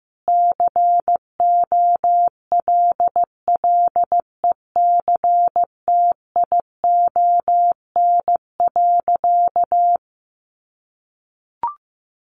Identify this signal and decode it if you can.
Morse code, heard as a keyed tone, about 15 wpm, 700 Hz. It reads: COLLECTION.
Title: Sentences from Top 1200 Words 15wpm